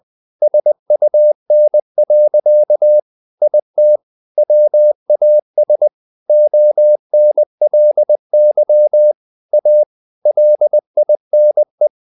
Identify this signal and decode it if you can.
Morse code, heard as a keyed tone, about 20 wpm, 600 Hz. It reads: SUN. IT WAS ONLY A LINE
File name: the old man and the sea / the old man and the sea part 1 20wpm